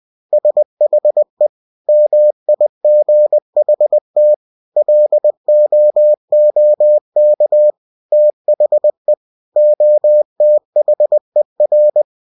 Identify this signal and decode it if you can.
Morse code, heard as a keyed tone, about 20 wpm, 600 Hz. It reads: SHE MIGHT LOOK THE OTHER
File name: cw-books / winnie-the-pooh / winnie-20wpm-page-0064